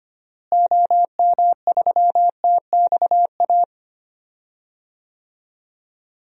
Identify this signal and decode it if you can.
Morse code, heard as a keyed tone, about 25 wpm, 700 Hz. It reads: OM3TXA
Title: DXCC-100 Call Signs - 6 Characters - Repeated 2x - 25wpm